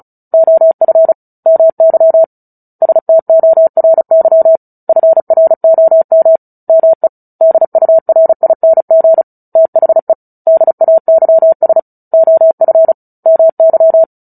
Morse code, heard as a keyed tone, about 34 wpm, 650 Hz. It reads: OF MY STORY FROM ME DURING THE DAYS OF MY
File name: Book Two / 09-Chapter 9 - 35 WPM